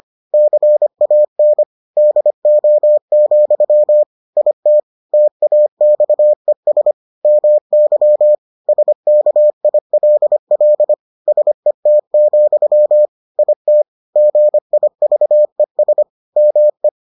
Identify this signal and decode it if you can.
Morse code, heard as a keyed tone, about 25 wpm, 600 Hz. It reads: CAN DO, IT TAXES MY SKILL SET, IT GIVES ME